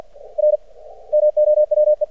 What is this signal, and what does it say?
Morse code, heard as a keyed tone, about 50 wpm, 600 Hz. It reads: R MOWE
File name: archived / CRSnet